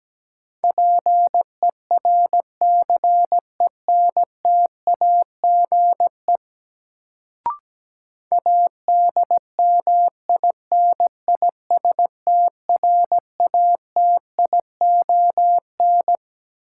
Morse code, heard as a keyed tone, about 17 wpm, 700 Hz. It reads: PERCENTAGE ADMINISTRATION
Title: Sentences from Top 1300 Words - New Word Drill - 17wpm